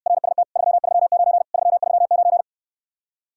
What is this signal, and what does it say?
Morse code, heard as a keyed tone, about 69 wpm, 700 Hz. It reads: RST 339 339